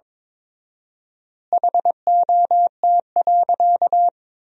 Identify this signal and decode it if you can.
Morse code, heard as a keyed tone, about 22 wpm, 700 Hz. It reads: HOT.
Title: 4x Word-Spacing - Sentences from Top 500 Words - 22wpm